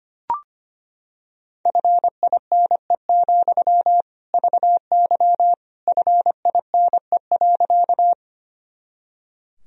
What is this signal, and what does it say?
Morse code, heard as a keyed tone, about 25 wpm, 700 Hz. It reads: FINE, VY FINE.
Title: Sentences from Top 500 Words and Abbreviations 25wpm